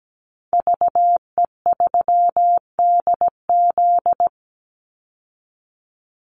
Morse code, heard as a keyed tone, about 17 wpm, 700 Hz. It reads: VE3DZ
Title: Call Signs Repeated 3x - 17wpm